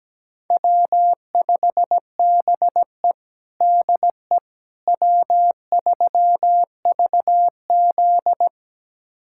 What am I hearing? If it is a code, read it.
Morse code, heard as a keyed tone, about 17 wpm, 700 Hz. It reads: W5BE DE W3VZ
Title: USA Callsign DE Callsign - 4 Characters - 17wpm